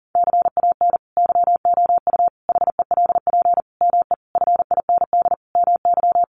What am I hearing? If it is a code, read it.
Morse code, heard as a keyed tone, about 37 wpm, 700 Hz. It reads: CAN YOU HELP ME FIND MY
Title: Sentences from Top 200 Words 40wpm